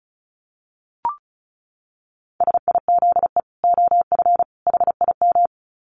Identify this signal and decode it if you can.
Morse code, heard as a keyed tone, about 34 wpm, 700 Hz. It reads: SIZE OF HIM
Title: Dit Challenge 35wpm